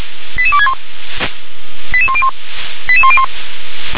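Five-tone selective call (ZVEI), heard as quick runs of five tones, tones of about 70 ms. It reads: 88261 88101 88101
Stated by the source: ZVEI five-tone call received by SDR.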